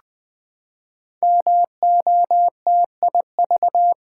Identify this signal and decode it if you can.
Morse code, heard as a keyed tone, about 20 wpm, 700 Hz. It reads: MOTIV
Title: Build A Word - 5 Letters At A Time - 10-Letter Words - 20wpm